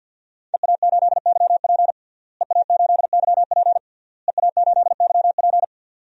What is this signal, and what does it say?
Morse code, heard as a keyed tone, about 50 wpm, 700 Hz. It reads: EA8YP EA8YP EA8YP
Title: DXCC-100 Call Signs - 5 Characters - Repeated 3x - 50wpm